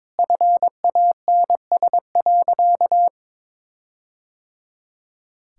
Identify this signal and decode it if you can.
Morse code, heard as a keyed tone, about 22 wpm, 700 Hz. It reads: FANS.
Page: morse-code-ninja-practice-22wpm